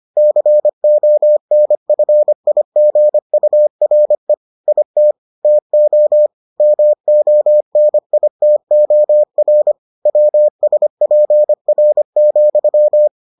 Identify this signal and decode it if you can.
Morse code, heard as a keyed tone, about 25 wpm, 600 Hz. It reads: CONFIGURE IT TO MONITOR WSPR,